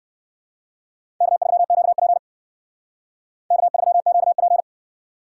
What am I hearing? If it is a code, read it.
Morse code, heard as a keyed tone, about 58 wpm, 700 Hz. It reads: K3QP K3QP